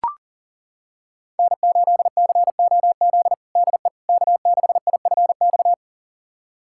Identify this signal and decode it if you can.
Morse code, heard as a keyed tone, about 40 wpm, 700 Hz. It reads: N8COZ DE K6IFX
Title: USA Callsign DE Callsign - 5 Characters - 40wpm